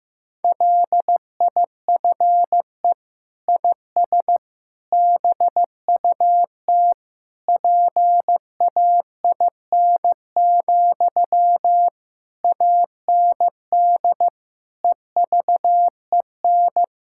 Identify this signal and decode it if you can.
Morse code, heard as a keyed tone, about 15 wpm, 700 Hz. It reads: LIFE IS BUT PAIN, AND EVEN
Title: Sentences from Top 1600 Words 15wpm